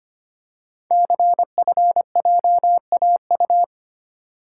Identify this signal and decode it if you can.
Morse code, heard as a keyed tone, about 25 wpm, 700 Hz. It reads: CFJAU
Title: Code Groups - 5 Letters - 25wpm